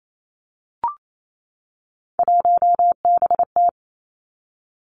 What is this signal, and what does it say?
Morse code, heard as a keyed tone, about 28 wpm, 700 Hz. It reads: 1BT
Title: Three Challenging Letter-Number Combinations 28wpm